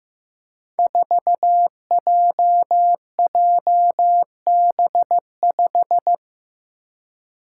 Morse code, heard as a keyed tone, about 15 wpm, 700 Hz. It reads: 4JJB5
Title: Code Groups - 5 Letter:Numbers - 15wpm